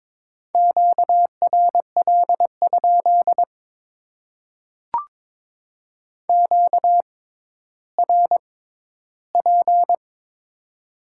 Morse code, heard as a keyed tone, about 22 wpm, spaced as for 12 wpm, 700 Hz. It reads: QRL? QRP
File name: Top 90 QSO Elements - 6x Character Spacing - 22wpm